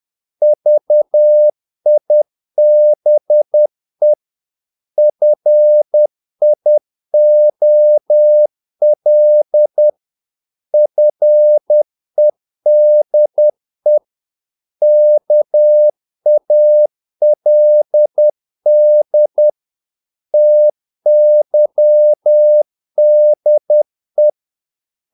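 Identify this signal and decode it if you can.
Morse code, heard as a keyed tone, about 10 wpm, 600 Hz. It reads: VIBE FIOL FEDE KALD TYDE
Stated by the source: Morse practice of short Danish words at 10 wpm, from CW med Gnister.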